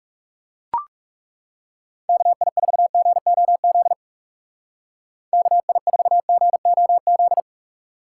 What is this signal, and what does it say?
Morse code, heard as a keyed tone, about 37 wpm, 700 Hz. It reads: KI4GOZ KI4GOZ
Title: Speed Racing - USA 2x3 Call Signs 30wpm